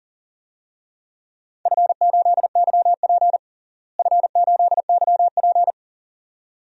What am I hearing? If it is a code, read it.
Morse code, heard as a keyed tone, about 40 wpm, 700 Hz. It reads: F8YP F8YP
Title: DXCC-100 Call Signs - 4 Characters - Repeated 2x - 40wpm